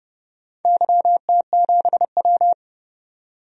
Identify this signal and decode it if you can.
Morse code, heard as a keyed tone, about 30 wpm, 700 Hz. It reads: YT7W
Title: DXCC-100 Call Signs - 4 Characters - 30wpm